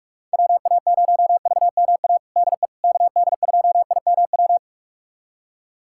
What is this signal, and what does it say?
Morse code, heard as a keyed tone, about 46 wpm, 700 Hz. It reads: WA0VMA DE KD2IMW
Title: USA Callsign DE Callsign 45wpm